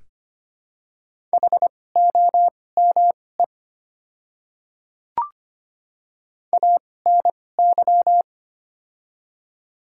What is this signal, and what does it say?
Morse code, heard as a keyed tone, about 25 wpm, spaced as for 18 wpm, 700 Hz. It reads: HOME ANY
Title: 2x Character Spacing - Top 200 Words - 25wpm